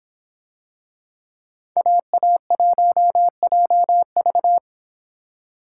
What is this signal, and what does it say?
Morse code, heard as a keyed tone, about 26 wpm, 700 Hz. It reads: AA1JV